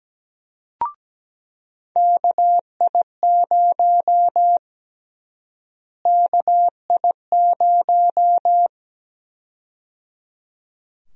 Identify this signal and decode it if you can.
Morse code, heard as a keyed tone, about 17 wpm, 700 Hz. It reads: KI0 KI0
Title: USA Call Sign Prefixes - Repeated 2x - 17wpm